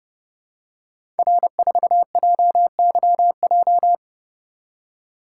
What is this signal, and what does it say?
Morse code, heard as a keyed tone, about 30 wpm, 700 Hz. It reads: R4JYJ
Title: Code Groups - 5 Letter:Numbers - 30wpm